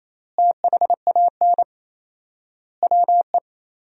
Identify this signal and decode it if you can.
Morse code, heard as a keyed tone, about 28 wpm, 700 Hz. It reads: THAN WE
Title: 4x Word-Spacing - Sentences from Top 100 Words - 28wpm